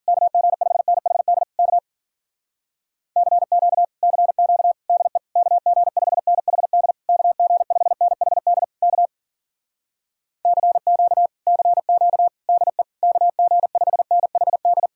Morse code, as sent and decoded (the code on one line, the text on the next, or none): -.- --. ..... -. .... -.. / -.- / -.-. --.- / -.-. --.- / -.. / -.- --. ..... -. .... -.. / -.- --. ..... -. .... -.. / -.- / -.-. --.- / -.-. --.- / -.. / -.- --. ..... -. .... -..
KG5NHD K CQ CQ DE KG5NHD KG5NHD K CQ CQ DE KG5NHD